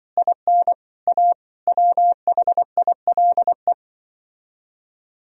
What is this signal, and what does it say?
Morse code, heard as a keyed tone, about 24 wpm, 700 Hz. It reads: IN A WHILE
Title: Speed-Racing - Sets of 5 Words 20wpm